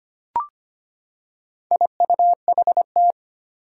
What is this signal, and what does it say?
Morse code, heard as a keyed tone, about 25 wpm, 700 Hz. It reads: IUHT
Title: Four Challenging Letter-Number Combinations 25wpm